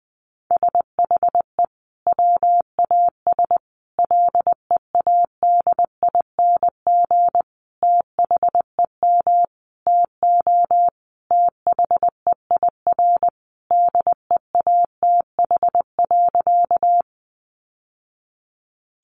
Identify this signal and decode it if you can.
Morse code, heard as a keyed tone, about 20 wpm, 700 Hz. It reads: SHE WAS LEADING THEM TO THEIR DEATH.
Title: Sentences from Top 1100 Words 20wpm